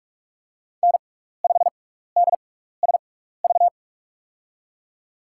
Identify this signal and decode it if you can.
Morse code, heard as a keyed tone, about 45 wpm, spaced as for 15 wpm, 700 Hz. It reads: N5DSV
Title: USA Call Signs - 6x Character Spacing - 45wpm